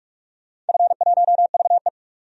Morse code, heard as a keyed tone, about 46 wpm, 700 Hz. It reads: F1VE
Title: DXCC-100 Call Signs - 4 Characters - 45wpm